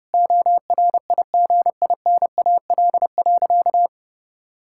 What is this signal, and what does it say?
Morse code, heard as a keyed tone, about 32 wpm, 700 Hz. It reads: ORIGINAL.
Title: Sentences from Top 1000 Words - Review Part 6 - 30wpm